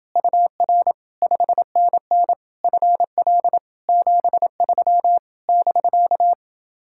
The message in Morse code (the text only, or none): UR 5NN FL 73 <BK>